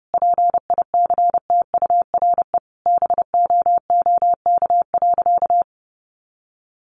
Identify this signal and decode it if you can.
Morse code, heard as a keyed tone, about 30 wpm, 700 Hz. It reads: PICTURE BOOK.
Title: Sentences from Top 500 Words 30wpm